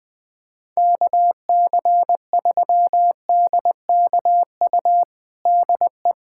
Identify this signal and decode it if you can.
Morse code, heard as a keyed tone, about 20 wpm, 700 Hz. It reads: KC3DKU DE
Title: USA Callsign DE Callsign 20wpm